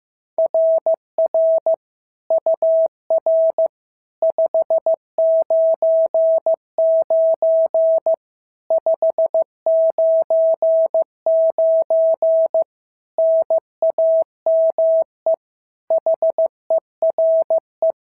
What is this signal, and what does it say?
Morse code, heard as a keyed tone, about 14 wpm, 650 Hz. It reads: RR UR 599 599 NAME HERE
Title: SOTA QSO Practice 15wpm